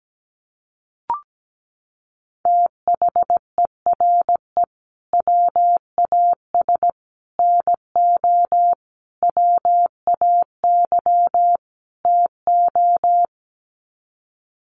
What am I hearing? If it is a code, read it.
Morse code, heard as a keyed tone, about 17 wpm, 700 Hz. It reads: THERE WAS NO WAY TO
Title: Sets of 5 Words Encore 17wpm